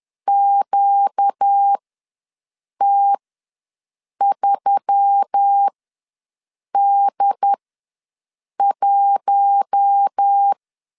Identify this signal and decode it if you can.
Morse code, heard as a keyed tone, about 10 wpm, spaced as for 6 wpm, 800 Hz. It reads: QT3D1
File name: LEZIONI / 6 PKX0 / 6 P ws 10wpm 3751